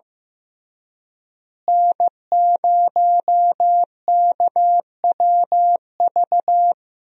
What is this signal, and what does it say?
Morse code, heard as a keyed tone, about 15 wpm, 700 Hz. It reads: N0KWV
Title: USA Call Signs - 5 Characters - Repeated 3x - 15wpm